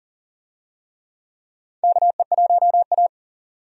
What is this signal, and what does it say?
Morse code, heard as a keyed tone, about 40 wpm, 700 Hz. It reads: KE1A
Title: USA Call Signs - 4 Characters - 40wpm